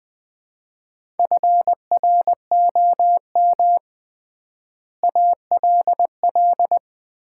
Morse code, heard as a keyed tone, about 20 wpm, 700 Hz. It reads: FROM ALL
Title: 3x Word-Spacing - Sets of 5 Words - 20wpm